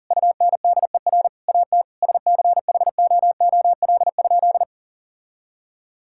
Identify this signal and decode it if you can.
Morse code, heard as a keyed tone, about 40 wpm, 700 Hz. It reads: UNDER AT SCHOOL?
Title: Sentences from Top 1000 Words - Review Part 6 - 40wpm